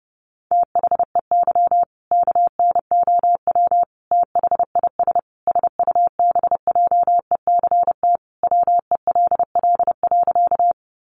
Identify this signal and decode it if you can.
Morse code, heard as a keyed tone, about 30 wpm, 700 Hz. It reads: THEY KNOW THIS SUBJECT WELL.
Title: Sentences from Top 800 Words 30wpm